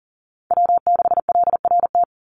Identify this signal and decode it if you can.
Morse code, heard as a keyed tone, about 40 wpm, 700 Hz. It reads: W6LRT
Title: USA Call Signs 40wpm